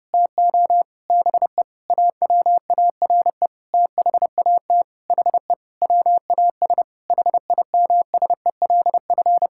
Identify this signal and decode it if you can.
Morse code, heard as a keyed tone, about 30 wpm, 700 Hz. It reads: TO BE AWARE THAT HE WAS HIMSELF
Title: Sentences from Top 1600 Words 30wpm